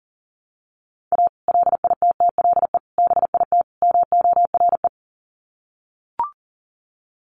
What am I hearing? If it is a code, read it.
Morse code, heard as a keyed tone, about 40 wpm, 700 Hz. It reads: A LITTLE BIT MORE
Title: Sets of 4 Words 40wpm